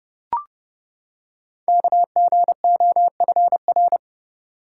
Morse code, heard as a keyed tone, about 29 wpm, 700 Hz. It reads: KGOFR
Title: Code Groups - 5 Letters - 30wpm